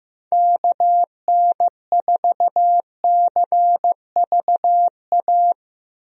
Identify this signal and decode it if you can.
Morse code, heard as a keyed tone, about 15 wpm, 700 Hz. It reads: KN4CVA
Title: USA Call Signs - 6 Characters - Repeated 2x - 15wpm